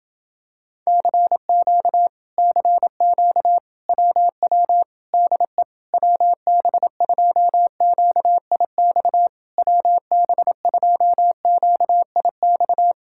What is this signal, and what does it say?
Morse code, heard as a keyed tone, about 27 wpm, 700 Hz. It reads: CQ CQ WW DE WB2QIX WB2QIX